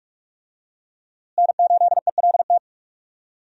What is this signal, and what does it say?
Morse code, heard as a keyed tone, about 46 wpm, 700 Hz. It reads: N8ELT
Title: USA Call Signs - 5 Characters - Repeated 3x - 45wpm